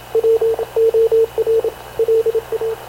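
Morse code, heard as a keyed tone, about 26 wpm, 450 Hz, with radio noise. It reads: W OR LA
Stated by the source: Havana Radio / CLA sending Spanish press (PX) in Morse, recorded by Radio Officer Paul Dumesnil onboard CNAV QUEST/CZDO.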